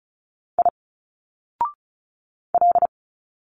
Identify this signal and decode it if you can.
Morse code, heard as a keyed tone, about 34 wpm, 700 Hz. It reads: I L
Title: Single Letters 35wpm